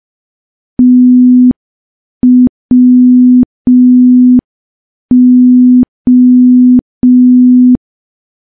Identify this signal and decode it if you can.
Morse code, heard as a keyed tone, about 5 wpm, 250 Hz. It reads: TWO
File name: morse / morse-training-05WPM